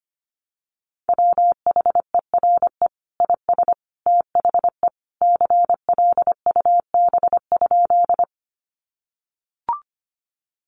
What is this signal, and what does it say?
Morse code, heard as a keyed tone, about 25 wpm, 700 Hz. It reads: WHERE IS THE CLUB?
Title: Sentences from Top 1200 Words 25wpm